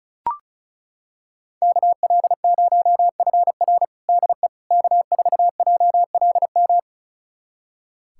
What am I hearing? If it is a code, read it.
Morse code, heard as a keyed tone, about 36 wpm, 700 Hz. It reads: KL0FR DE K4JLM
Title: USA Callsign DE Callsign - 5 Characters - 35wpm